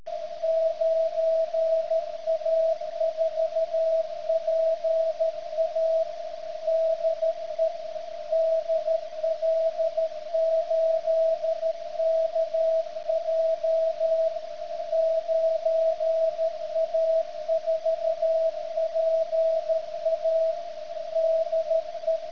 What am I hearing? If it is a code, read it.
Morse code, heard as a keyed tone, about 13 wpm, 650 Hz. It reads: 9A4PA DE DL8KJ 9A4PA DE
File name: ft8-vs-cw / 100-to-150 CW 100hz